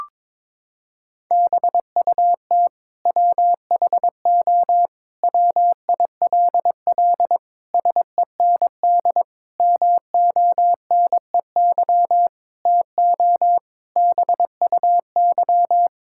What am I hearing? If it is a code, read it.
Morse code, heard as a keyed tone, about 22 wpm, 700 Hz. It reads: BUT WHO WILL SEND MONEY TO BUY
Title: Sentences from Top 1600 Words 22wpm